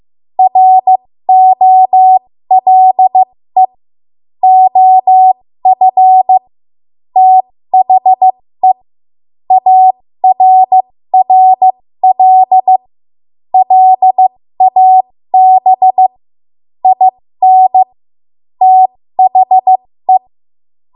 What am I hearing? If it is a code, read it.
Morse code, heard as a keyed tone, about 15 wpm, spaced as for 13 wpm, 750 Hz. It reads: ROLE OF THE ARRL LAB IN THE
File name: Morse / 13WPM